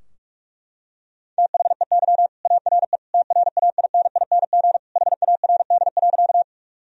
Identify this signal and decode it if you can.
Morse code, heard as a keyed tone, about 41 wpm, 700 Hz. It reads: THEY ARE TRAINING HARD.
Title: Sentences from Top 1000 Words - Review Part 4 - 45wpm